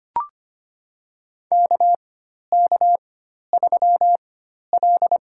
Morse code, heard as a keyed tone, about 25 wpm, spaced as for 12 wpm, 700 Hz. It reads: KK3L
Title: USA Call Signs - 4 Characters - 4x Character Spacing - 25wpm